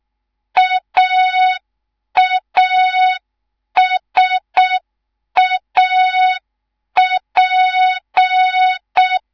Morse code, heard as a keyed tone, about 6 wpm, 3700 Hz. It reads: AASAP
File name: Morze-HA2MN / hang / 10GWZ HA2MN